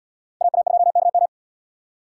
Morse code, heard as a keyed tone, about 72 wpm, 700 Hz. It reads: AA2KM